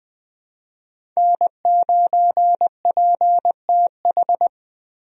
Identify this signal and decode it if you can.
Morse code, heard as a keyed tone, about 20 wpm, 700 Hz. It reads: N9PTH
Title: USA Call Signs - 5 Characters - Repeated 2x - 20wpm